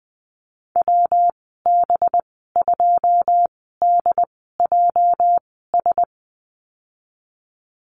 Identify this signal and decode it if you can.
Morse code, heard as a keyed tone, about 20 wpm, spaced as for 15 wpm, 700 Hz. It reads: WB2DJS
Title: 219 Lesson 35g - Number 0 - Call Signs 20wpm